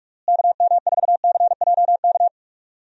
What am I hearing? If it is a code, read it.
Morse code, heard as a keyed tone, about 46 wpm, 700 Hz. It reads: KM4CJK
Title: USA Call Signs - 6 Characters - Repeated 3x - 45wpm